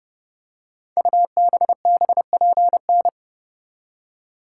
Morse code, heard as a keyed tone, about 30 wpm, 700 Hz. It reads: UBBPN
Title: Code Groups - 5 Letter:Numbers - 30wpm